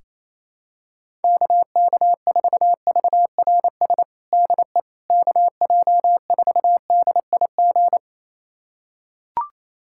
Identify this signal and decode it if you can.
Morse code, heard as a keyed tone, about 28 wpm, 700 Hz. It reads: KK4VRS DE KJ4DIG
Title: USA Callsign DE Callsign - 6 Characters - 28wpm